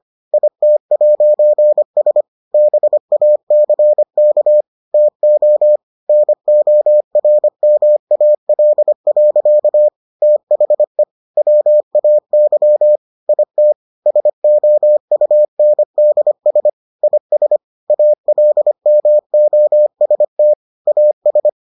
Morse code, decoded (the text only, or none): IT'S BACK TO NORMAL. THE WAY IT SOUNDS IS ALMOST AS